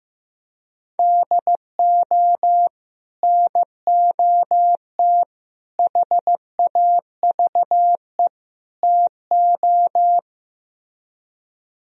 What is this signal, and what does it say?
Morse code, heard as a keyed tone, about 16 wpm, 700 Hz. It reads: DO NOT HAVE TO
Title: Speed-Racing - Sets of 4 Words 15wpm